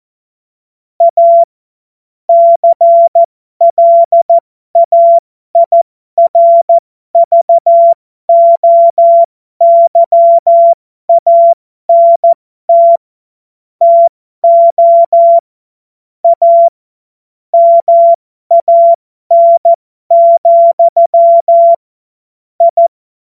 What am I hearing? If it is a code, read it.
Morse code, heard as a keyed tone, about 14 wpm, spaced as for 12 wpm, 680 Hz. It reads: A CLAIRVOYANT TO A MAN, I
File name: Morsekurs / Witze